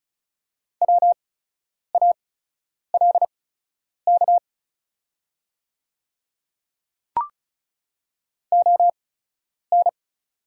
Morse code, heard as a keyed tone, about 36 wpm, spaced as for 10 wpm, 700 Hz. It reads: WALK ON
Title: 8x Character Spacing - Top 300 Words - 35wpm